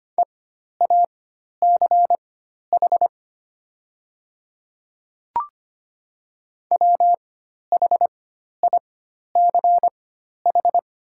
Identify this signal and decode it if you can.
Morse code, heard as a keyed tone, about 25 wpm, spaced as for 12 wpm, 700 Hz. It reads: EACH WHICH
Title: 4x Character Spacing - Top 100 Words - 25wpm